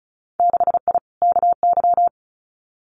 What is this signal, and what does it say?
Morse code, heard as a keyed tone, about 34 wpm, 700 Hz. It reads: 6I KY
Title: Winter Field Day Exchanges 35wpm